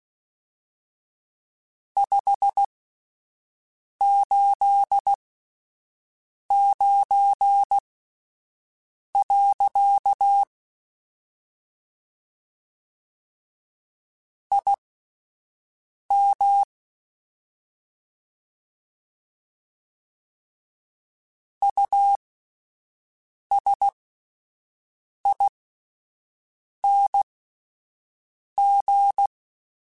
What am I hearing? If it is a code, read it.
Morse code, heard as a keyed tone, about 16 wpm, spaced as for 5 wpm, 800 Hz. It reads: 589. IM USING
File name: (spaced) / QSOs Spaced-mp3s / WD0FBT 1 @10 sp